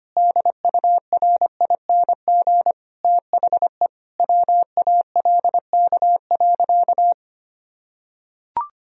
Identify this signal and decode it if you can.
Morse code, heard as a keyed tone, about 25 wpm, 700 Hz. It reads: DURING THE WALK.
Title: Sentences from Top 1000 Words - Review Part 2 - 25wpm